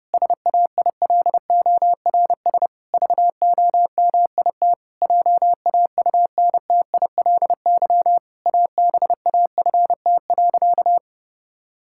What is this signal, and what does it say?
Morse code, heard as a keyed tone, about 30 wpm, 700 Hz. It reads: SAILORS VOMIT JAUNTILY ABAFT.
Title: Pangram Challenge v2 - 30wpm